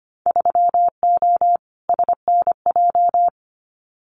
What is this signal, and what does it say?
Morse code, heard as a keyed tone, about 25 wpm, 700 Hz. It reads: 3O SNJ
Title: Winter Field Day Exchanges Encore 25wpm